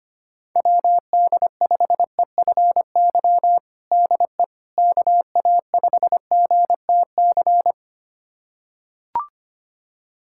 Morse code, heard as a keyed tone, about 25 wpm, 700 Hz. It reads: WD5EFY DE KA5GTC
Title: USA Callsign DE Callsign - 6 Characters - 25wpm